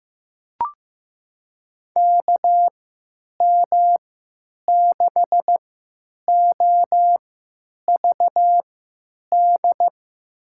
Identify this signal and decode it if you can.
Morse code, heard as a keyed tone, about 15 wpm, spaced as for 9 wpm, 700 Hz. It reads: KM6OVD
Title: USA Call Signs - 6 Characters - 3x Character Spacing - 15wpm